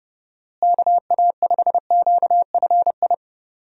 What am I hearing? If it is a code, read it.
Morse code, heard as a keyed tone, about 30 wpm, 700 Hz. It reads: KA5QFI
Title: USA Call Signs - 6 Characters - Repeated 3x - 30wpm